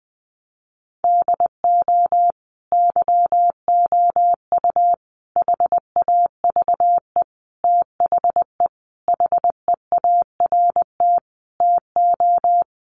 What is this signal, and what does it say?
Morse code, heard as a keyed tone, about 20 wpm, 700 Hz. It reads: DO YOU HAVE THE HEART TO
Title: Sentences from Top 1200 Words 20wpm